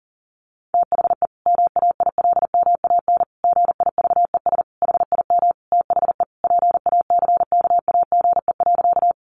Morse code, heard as a keyed tone, about 40 wpm, 700 Hz. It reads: THE MAILMAN GIVES HIM THE PACKAGE.
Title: Additional Easy Phrases 40wpm